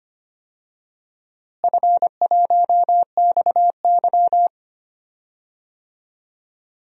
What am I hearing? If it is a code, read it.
Morse code, heard as a keyed tone, about 25 wpm, 700 Hz. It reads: F1XY
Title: DXCC-100 Call Signs - 4 Characters - Repeated 2x - 25wpm